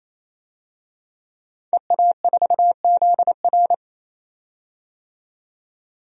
Morse code, heard as a keyed tone, about 28 wpm, 700 Hz. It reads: EA4ZR
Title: DXCC-100 Call Signs - 5 Characters - 28wpm